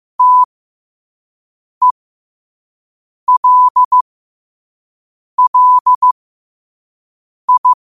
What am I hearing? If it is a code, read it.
Morse code, heard as a keyed tone, about 15 wpm, spaced as for 5 wpm, 1000 Hz. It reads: TELLI
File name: qotdfiles / 5wpm